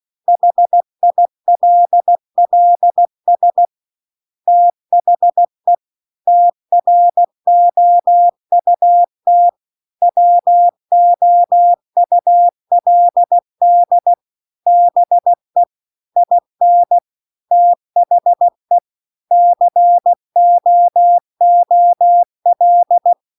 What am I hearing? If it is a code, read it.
Morse code, heard as a keyed tone, about 16 wpm, 700 Hz. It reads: HILLS THE TROUT WOULD BE IN THE COOL